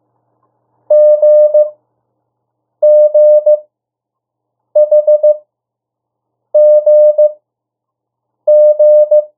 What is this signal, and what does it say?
Morse code, heard as a keyed tone, about 15 wpm, spaced as for 6 wpm, 600 Hz. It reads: GGHGG
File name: urok 1